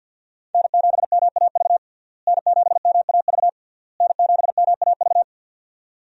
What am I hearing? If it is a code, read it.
Morse code, heard as a keyed tone, about 53 wpm, 700 Hz. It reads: N7MAV N7MAV N7MAV
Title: USA Call Signs - 5 Characters - Repeated 3x - 50wpm